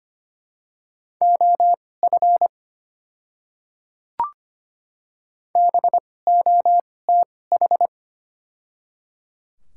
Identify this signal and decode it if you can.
Morse code, heard as a keyed tone, about 25 wpm, spaced as for 18 wpm, 700 Hz. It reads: OF BOTH
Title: 2x Character Spacing - Top 500 Words - 25wpm